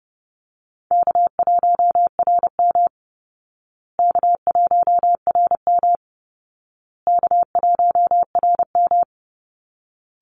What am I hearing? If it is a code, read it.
Morse code, heard as a keyed tone, about 30 wpm, 700 Hz. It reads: K1RM K1RM K1RM
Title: Call Signs Repeated 2x - 30wpm